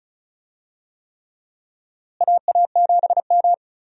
Morse code, heard as a keyed tone, about 36 wpm, 700 Hz. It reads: AA7M
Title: USA Call Signs - 4 Characters - Repeated 3x - 35wpm